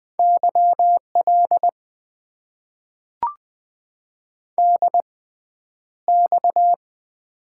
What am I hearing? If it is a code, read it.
Morse code, heard as a keyed tone, about 20 wpm, spaced as for 10 wpm, 700 Hz. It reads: YL DX
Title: Top 90 QSO Elements - 6x Character Spacing - 20wpm